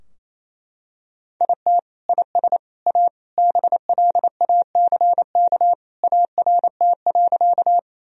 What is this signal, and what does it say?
Morse code, heard as a keyed tone, about 27 wpm, 700 Hz. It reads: IT IS A BLACK ART.
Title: Sentences from Top 1000 Words - Review Part 3 - 28wpm